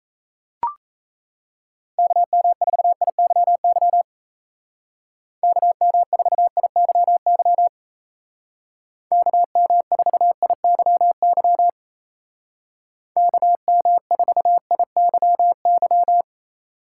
Morse code, heard as a keyed tone, about 33 wpm, 700 Hz. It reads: KM4IYY KM4IYY KM4IYY KM4IYY
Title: Speed Racing - USA Call Signs 28wpm